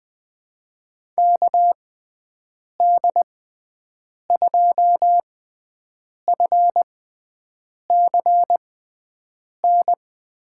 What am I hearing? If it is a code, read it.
Morse code, heard as a keyed tone, about 20 wpm, spaced as for 7 wpm, 700 Hz. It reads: KD2FCN
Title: USA Call Signs - 6x Character Spacing - 20wpm